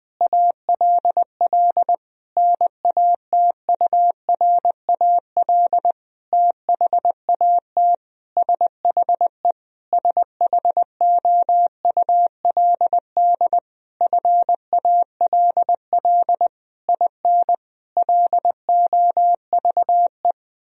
Morse code, heard as a keyed tone, about 21 wpm, 700 Hz. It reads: ALL NATURAL THAT SHE SHOULD FALL IN LOVE
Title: Sentences from Top 1000 Words - Review Part 4 - 20wpm